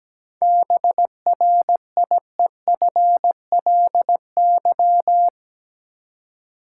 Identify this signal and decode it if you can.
Morse code, heard as a keyed tone, about 17 wpm, 700 Hz. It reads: BRIEFLY
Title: Seven Letter Words - 17wpm